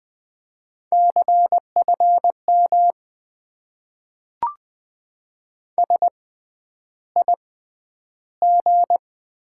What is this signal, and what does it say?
Morse code, heard as a keyed tone, about 20 wpm, spaced as for 10 wpm, 700 Hz. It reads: CFM SIG
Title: Top 90 QSO Elements - 6x Character Spacing - 20wpm